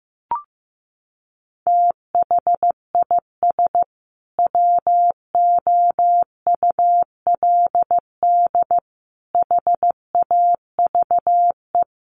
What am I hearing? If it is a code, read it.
Morse code, heard as a keyed tone, about 15 wpm, 700 Hz. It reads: THIS WOULD HAVE
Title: Sentences from Top 800 Words 15wpm